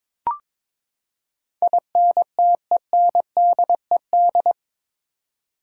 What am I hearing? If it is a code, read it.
Morse code, heard as a keyed tone, about 22 wpm, 700 Hz. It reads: INTENDED
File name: practice / 22wpm / Eight Letter Words - 22wpm